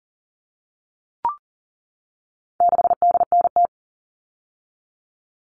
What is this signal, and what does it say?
Morse code, heard as a keyed tone, about 40 wpm, 700 Hz. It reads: 6DNT
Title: All Cut Numbers - 4-Digit 40wpm